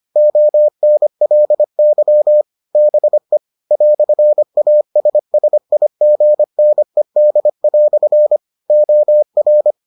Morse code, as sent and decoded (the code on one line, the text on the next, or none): --- -. .-.. -.-- / -... / .-..-. .- ... ... .. --. -. -.. .-..-. / --- .-.
ONLY BE "ASSIGNED" OR